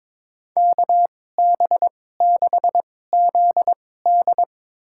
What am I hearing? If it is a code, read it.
Morse code, heard as a keyed tone, about 22 wpm, spaced as for 16 wpm, 700 Hz. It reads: KB6ZD
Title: USA Call Signs - 2x Character Spacing - 22wpm